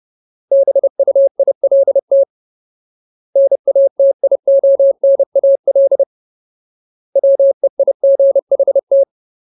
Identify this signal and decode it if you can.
Morse code, heard as a keyed tone, about 30 wpm, 550 Hz. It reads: BUILT NATIONAL WEIGHT
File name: Ultimate Word List / 30WPM Ultimate